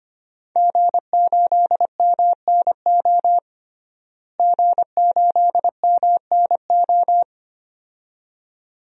Morse code, heard as a keyed tone, about 25 wpm, 700 Hz. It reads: G8MNO G8MNO
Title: DXCC-100 Call Signs - 5 Characters - Repeated 2x - 25wpm